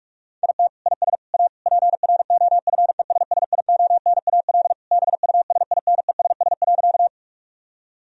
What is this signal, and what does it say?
Morse code, heard as a keyed tone, about 46 wpm, 700 Hz. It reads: IT IS A PROFESSIONAL BUSINESS.
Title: Sentences from Top 1000 Words - Review Part 4 - 45wpm